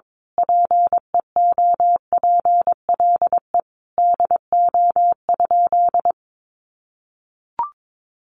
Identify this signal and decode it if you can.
Morse code, heard as a keyed tone, about 22 wpm, 700 Hz. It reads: PEOPLE DO?
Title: Sentences from Top 1100 Words 22wpm